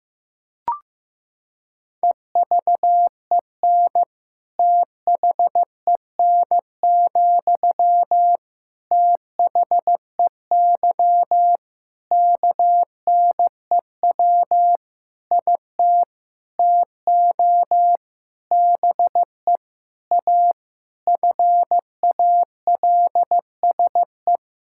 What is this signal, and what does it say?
Morse code, heard as a keyed tone, about 15 wpm, 700 Hz. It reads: EVEN THEN, THEY KNEW IT TO BE A FALSE
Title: Sentences from Top 1600 Words 15wpm